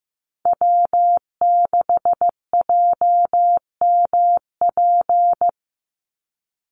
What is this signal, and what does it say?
Morse code, heard as a keyed tone, about 15 wpm, 700 Hz. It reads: W6JMP
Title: NA SOTA Chaser Callsigns 15wpm